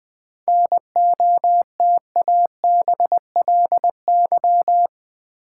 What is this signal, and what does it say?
Morse code, heard as a keyed tone, about 20 wpm, 700 Hz. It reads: NOTABLY
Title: 7 Letter Words - First 3 Letters Given - 20wpm